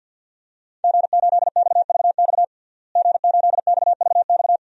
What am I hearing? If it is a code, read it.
Morse code, heard as a keyed tone, about 51 wpm, 700 Hz. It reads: G8XVX G8XVX
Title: DXCC-100 Call Signs - 5 Characters - Repeated 2x - 50wpm